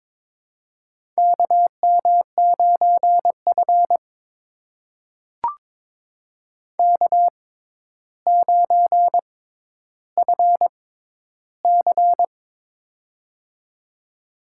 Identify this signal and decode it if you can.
Morse code, heard as a keyed tone, about 22 wpm, spaced as for 11 wpm, 700 Hz. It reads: KM9F K9FC
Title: USA Call Signs - 4 Characters - 6x Character Spacing - 22wpm